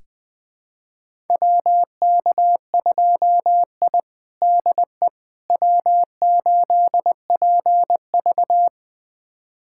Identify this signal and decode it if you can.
Morse code, heard as a keyed tone, about 20 wpm, 700 Hz. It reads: WK2I DE W8PV
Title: USA Callsign DE Callsign - 4 Characters - 20wpm